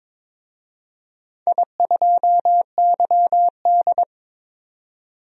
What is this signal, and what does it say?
Morse code, heard as a keyed tone, about 22 wpm, 700 Hz. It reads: I2YD
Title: DXCC-100 Call Signs - 4 Characters - 22wpm